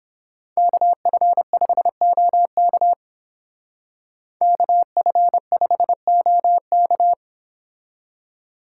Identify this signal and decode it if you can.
Morse code, heard as a keyed tone, about 30 wpm, 700 Hz. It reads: KF5OK KF5OK